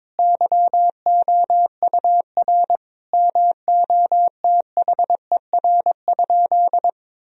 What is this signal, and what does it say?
Morse code, heard as a keyed tone, about 22 wpm, 700 Hz. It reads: YOUR MOTHER?
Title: Sentences from Top 1000 Words - Review Part 2 - 22wpm